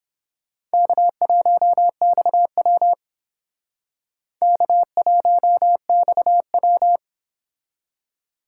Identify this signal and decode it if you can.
Morse code, heard as a keyed tone, about 31 wpm, 700 Hz. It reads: K1XW K1XW